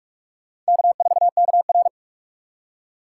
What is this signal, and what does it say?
Morse code, heard as a keyed tone, about 66 wpm, spaced as for 55 wpm, 700 Hz. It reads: K4KR
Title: Speed Racing - USA 1x2 Call Signs 45wpm